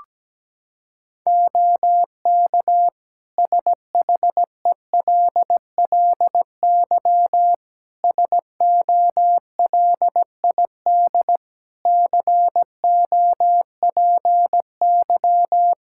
Morse code, heard as a keyed tone, about 17 wpm, 700 Hz. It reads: OK SHELLY SOLID COPY
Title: Practice QSO - Exchange 3 - More Info - 17wpm